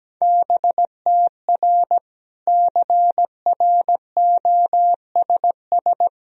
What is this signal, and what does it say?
Morse code, heard as a keyed tone, about 17 wpm, 700 Hz. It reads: BTR CROSS
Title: Sentences from Top 400 Words and Abbreviations 17wpm